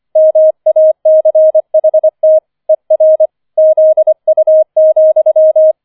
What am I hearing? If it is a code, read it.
Morse code, heard as a keyed tone, about 25 wpm, 630 Hz. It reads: MACHT ER ZU,
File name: 005 Fünfter Streich 25wpm